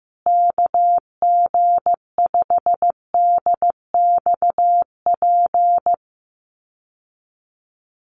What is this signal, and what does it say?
Morse code, heard as a keyed tone, about 15 wpm, 700 Hz. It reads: KG5DXP
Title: USA Call Signs Repeated 3x 15wpm